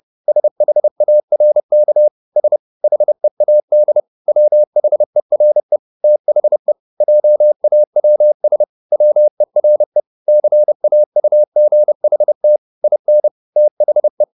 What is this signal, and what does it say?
Morse code, heard as a keyed tone, about 30 wpm, 600 Hz. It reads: SHARK S HEAD WHERE THE JAWS WERE CAUGHT IN THE